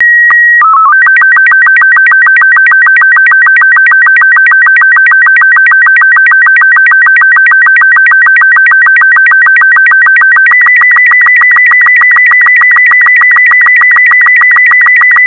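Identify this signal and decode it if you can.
SSTV (slow-scan television), Robot 36.
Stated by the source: Slow-scan television signal in a cybersecurity CTF challenge, of the kind the International Space Station reportedly transmits.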